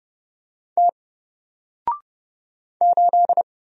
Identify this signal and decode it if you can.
Morse code, heard as a keyed tone, about 30 wpm, 700 Hz. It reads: T 8
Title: Single Character (Letter:Number:Punctuation:Prosign) 30wpm